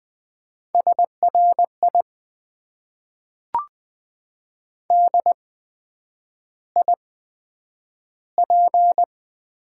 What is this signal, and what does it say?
Morse code, heard as a keyed tone, about 20 wpm, spaced as for 9 wpm, 700 Hz. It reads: SRI DIP
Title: Top 90 QSO Elements - 8x Character Spacing - 20wpm